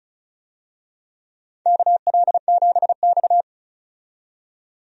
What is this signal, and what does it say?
Morse code, heard as a keyed tone, about 35 wpm, 700 Hz. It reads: KL7X
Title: DXCC-100 Call Signs - 4 Characters - 35wpm